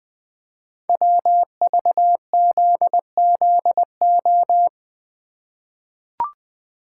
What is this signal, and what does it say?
Morse code, heard as a keyed tone, about 20 wpm, 700 Hz. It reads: WVZZO
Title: Code Groups - 5 Letters - 20wpm